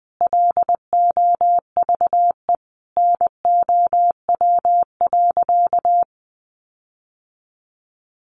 Morse code, heard as a keyed tone, about 19 wpm, 700 Hz. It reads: LOVE NOW.
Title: Sentences from Top 1300 Words 20wpm